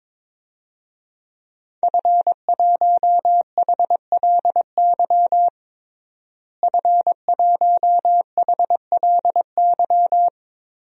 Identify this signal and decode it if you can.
Morse code, heard as a keyed tone, about 22 wpm, 700 Hz. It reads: F1HLY F1HLY
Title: DXCC-100 Call Signs - 5 Characters - Repeated 3x - 22wpm